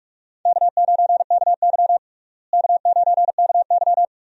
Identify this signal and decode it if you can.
Morse code, heard as a keyed tone, about 41 wpm, 700 Hz. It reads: K9KY K9KY
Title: USA Call Signs - 4 Characters - Repeated 2x - 45wpm